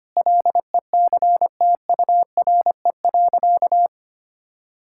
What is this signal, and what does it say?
Morse code, heard as a keyed tone, about 25 wpm, 700 Hz. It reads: LECTURE.
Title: Sentences from Top 1600 Words 25wpm